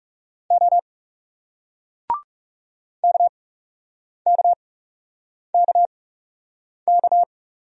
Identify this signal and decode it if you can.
Morse code, heard as a keyed tone, about 40 wpm, spaced as for 8 wpm, 700 Hz. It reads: O KKKK